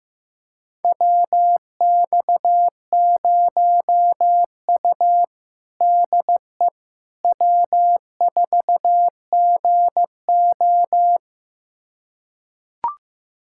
Morse code, heard as a keyed tone, about 15 wpm, 700 Hz. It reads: WX0U DE W4GO
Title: USA Callsign DE Callsign - 4 Characters - 15wpm